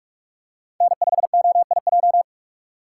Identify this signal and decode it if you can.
Morse code, heard as a keyed tone, about 44 wpm, 700 Hz. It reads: N5OIJ